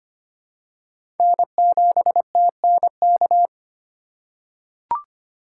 Morse code, heard as a keyed tone, about 25 wpm, 700 Hz. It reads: N7TNK
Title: USA Call Signs - 5 Characters - 2x Character Spacing - 25wpm